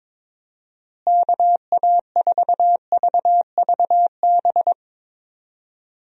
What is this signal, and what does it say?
Morse code, heard as a keyed tone, about 22 wpm, 700 Hz. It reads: KA4VVB
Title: USA Call Signs - 6 Characters - Repeated 2x - 22wpm